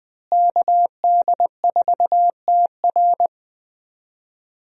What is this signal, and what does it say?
Morse code, heard as a keyed tone, about 21 wpm, 700 Hz. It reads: KD4TR
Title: USA Call Signs - 5 Characters - Repeated 3x - 20wpm